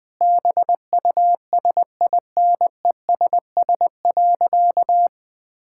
Morse code, heard as a keyed tone, about 21 wpm, 700 Hz. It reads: BUSINESS.
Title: Sentences from Top 1000 Words - Review Part 6 - 20wpm